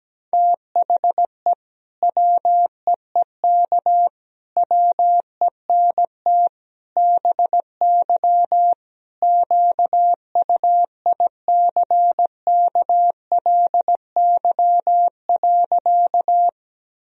Morse code, heard as a keyed tone, about 17 wpm, 700 Hz. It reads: THE WEEK WENT BY QUICKLY.
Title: Sentences from Top 1000 Words - Review Part 2 - 17wpm